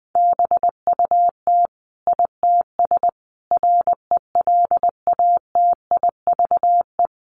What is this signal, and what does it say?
Morse code, heard as a keyed tone, about 20 wpm, 700 Hz. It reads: BUT ITS RELATIVE
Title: Sentences from Top 1400 Words 20wpm